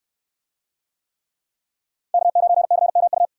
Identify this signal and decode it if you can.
Morse code, heard as a keyed tone, about 69 wpm, 700 Hz. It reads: K9CMU